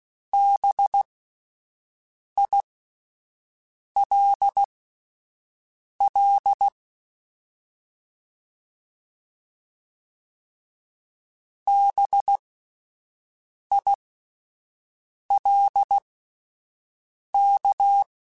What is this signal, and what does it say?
Morse code, heard as a keyed tone, about 16 wpm, spaced as for 5 wpm, 800 Hz. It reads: BILL BILK
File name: Spaced Alphas / Spaced Alphas mp3s / Bs sp @10